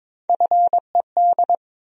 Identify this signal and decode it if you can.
Morse code, heard as a keyed tone, about 22 wpm, 700 Hz. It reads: FED
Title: USA Call Sign Postfixes - 3 Letters 22wpm